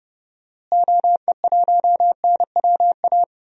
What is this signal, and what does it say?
Morse code, heard as a keyed tone, about 31 wpm, 700 Hz. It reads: OE1NWA
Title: DXCC-100 Call Signs - 6 Characters - Repeated 2x - 30wpm